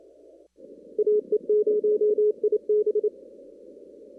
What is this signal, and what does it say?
Morse code, heard as a keyed tone, about 27 wpm, 420 Hz. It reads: AE0IB
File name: CW / kt5j wpx cw-op-k5pi